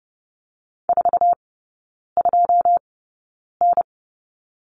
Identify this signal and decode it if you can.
Morse code, heard as a keyed tone, about 30 wpm, spaced as for 9 wpm, 700 Hz. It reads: 42N
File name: 081 Lesson 14e - Number 2 - Three Characters 30wpm